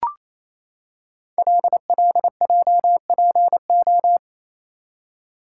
Morse code, heard as a keyed tone, about 27 wpm, 690 Hz. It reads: LLJPO
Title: Code Groups - 5 Letters - 28wpm